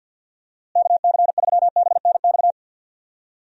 Morse code, heard as a keyed tone, about 47 wpm, 700 Hz. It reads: KC3BNX
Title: USA Call Signs - 6 Characters - Repeated 2x - 50wpm